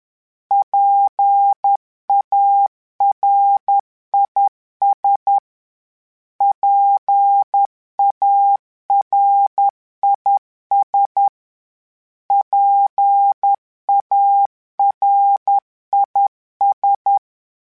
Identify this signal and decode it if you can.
Morse code, heard as a keyed tone, about 10 wpm, 800 Hz. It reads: PARIS PARIS PARIS